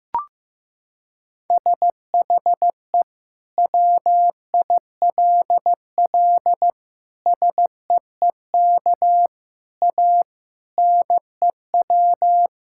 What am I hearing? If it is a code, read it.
Morse code, heard as a keyed tone, about 15 wpm, 700 Hz. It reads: SHE WILL SEEK A NEW
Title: Sentences from Top 1600 Words 15wpm